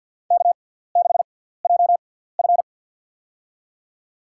Morse code, heard as a keyed tone, about 52 wpm, spaced as for 17 wpm, 700 Hz. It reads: K6JF
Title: USA Call Signs - 4 Characters - 6x Character Spacing - 50wpm